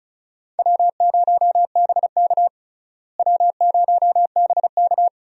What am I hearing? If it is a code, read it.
Morse code, heard as a keyed tone, about 36 wpm, 700 Hz. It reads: W0BK W0BK
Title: USA Call Signs - 4 Characters - Repeated 2x - 35wpm